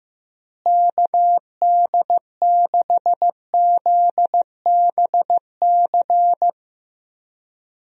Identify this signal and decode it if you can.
Morse code, heard as a keyed tone, about 15 wpm, 700 Hz. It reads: KD6ZBC
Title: USA Call Signs - 6 Characters - Repeated 3x - 15wpm